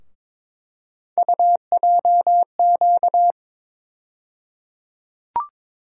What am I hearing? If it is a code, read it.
Morse code, heard as a keyed tone, about 22 wpm, 700 Hz. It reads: UJQ
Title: USA Call Sign Postfixes - 3 Letters 22wpm